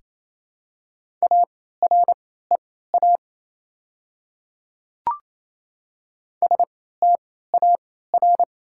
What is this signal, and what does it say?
Morse code, heard as a keyed tone, about 28 wpm, spaced as for 16 wpm, 690 Hz. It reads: AREA STAR
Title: 3x Character Spacing - Top 500 Words - 28wpm